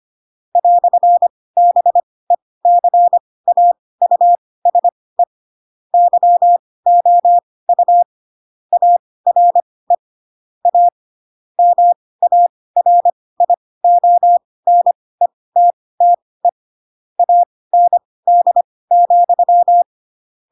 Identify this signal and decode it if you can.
Morse code, heard as a keyed tone, about 25 wpm, spaced as for 18 wpm, 700 Hz. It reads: "BECAUSE YOU ARE A MARIONETTE AND,
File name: Pinocchio / Pinocchio ch4 12 25 ik1hge